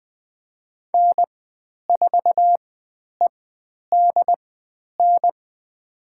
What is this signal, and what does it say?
Morse code, heard as a keyed tone, about 20 wpm, spaced as for 10 wpm, 700 Hz. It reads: N4EDN
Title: 095 Lesson 16g - Letter c - Call Signs 20wpm